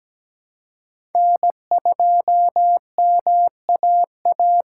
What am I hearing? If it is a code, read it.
Morse code, heard as a keyed tone, about 17 wpm, 700 Hz. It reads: N2MAA
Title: USA Call Signs - 2x Character Spacing - 17wpm